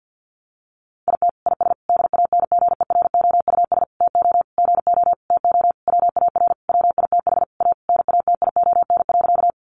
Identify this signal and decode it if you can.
Morse code, heard as a keyed tone, about 50 wpm, 700 Hz. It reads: IT IS DANGEROUS TO GO TO WAR WITH A NATION.
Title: Sentences from Top 1200 Words 50wpm